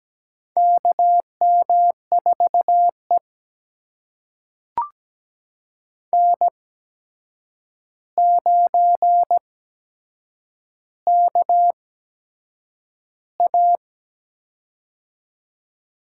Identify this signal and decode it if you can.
Morse code, heard as a keyed tone, about 17 wpm, spaced as for 7 wpm, 700 Hz. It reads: KM4E N9KA
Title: USA Call Signs - 4 Characters - 8x Character Spacing - 17wpm